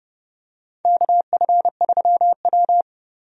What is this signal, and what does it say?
Morse code, heard as a keyed tone, about 32 wpm, 700 Hz. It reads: KF3W
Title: USA Call Signs - 4 Characters - 30wpm